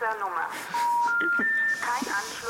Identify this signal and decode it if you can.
Special information tone, with speech.